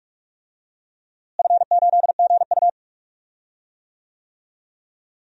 Morse code, heard as a keyed tone, about 44 wpm, 700 Hz. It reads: F8GU
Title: DXCC-100 Call Signs - 4 Characters - Repeated 2x - 45wpm